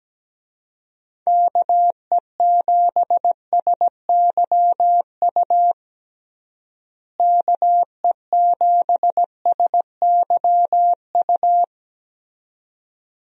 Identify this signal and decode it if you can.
Morse code, heard as a keyed tone, about 17 wpm, 700 Hz. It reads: KE7SYU KE7SYU
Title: USA Call Signs - 6 Characters - Repeated 2x - 17wpm